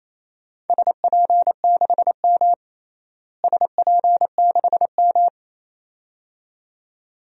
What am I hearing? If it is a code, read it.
Morse code, heard as a keyed tone, about 28 wpm, 700 Hz. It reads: SP6M SP6M
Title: DXCC-100 Call Signs - 4 Characters - Repeated 2x - 28wpm